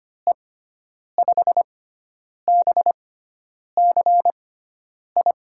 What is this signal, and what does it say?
Morse code, heard as a keyed tone, about 25 wpm, spaced as for 9 wpm, 700 Hz. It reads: E5BCI
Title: USA Call Signs - 6 Characters - 6x Character Spacing - 25wpm